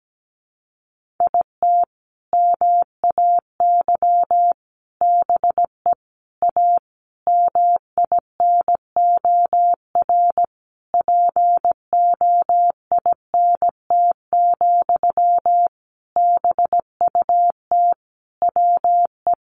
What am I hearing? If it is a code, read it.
Morse code, heard as a keyed tone, about 17 wpm, 700 Hz. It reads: IT MAY BE A MINOR POINT, BUT WE
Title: Sentences from Top 1500 Words 17wpm